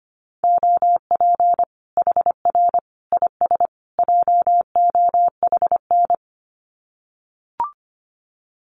Morse code, heard as a keyed tone, about 25 wpm, 700 Hz. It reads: OP HR IS JOHN
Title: Common QSO Phrases 25wpm